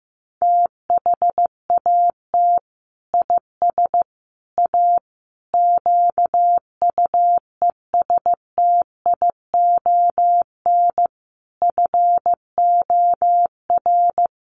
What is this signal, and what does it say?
Morse code, heard as a keyed tone, about 14 wpm, 700 Hz. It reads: THAT IS A QUESTION FOR
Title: Sentences from Top 600 Words 15wpm